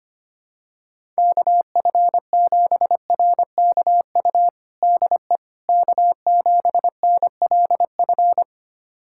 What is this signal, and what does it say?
Morse code, heard as a keyed tone, about 25 wpm, 700 Hz. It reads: KF7RKU DE K7NLF
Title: USA Callsign DE Callsign 25wpm